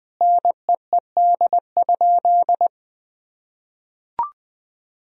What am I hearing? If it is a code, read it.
Morse code, heard as a keyed tone, about 21 wpm, 700 Hz. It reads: NEED?
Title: Sentences from Top 1000 Words - Review Part 4 - 20wpm